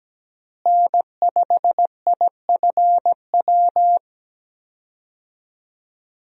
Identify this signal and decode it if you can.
Morse code, heard as a keyed tone, about 17 wpm, 700 Hz. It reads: N5IFW
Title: USA Call Signs - 5 Characters - Repeated 2x - 17wpm